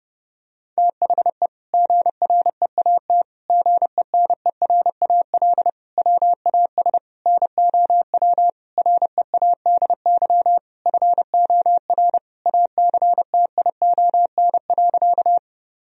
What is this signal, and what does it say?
Morse code, heard as a keyed tone, about 32 wpm, 700 Hz. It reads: THE GREAT GENERAL WAS NOW READY FOR ACTION.
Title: Sentences from Top 1000 Words - Review Part 4 - 30wpm